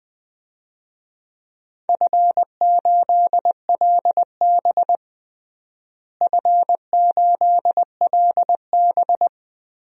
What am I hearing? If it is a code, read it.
Morse code, heard as a keyed tone, about 21 wpm, 700 Hz. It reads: F8LB F8LB
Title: DXCC-100 Call Signs - 4 Characters - Repeated 3x - 20wpm